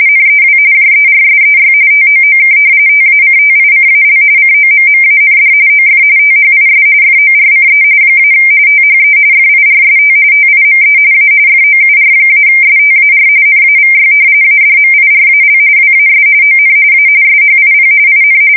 RTTY, about 45.45 baud. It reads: NEW JERSEY TOOK TACTICAL COMMAND OF THIS FORCE - THE FIRST TIME A FOUR-STAR NAVY ADMIRAL WAS ON A SHIP IN
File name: fsk170 / battleshipnj